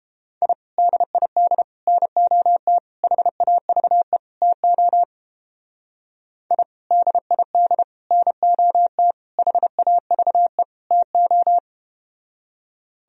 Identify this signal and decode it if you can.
Morse code, heard as a keyed tone, about 32 wpm, 700 Hz. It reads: I DID NOT HAVE TO I DID NOT HAVE TO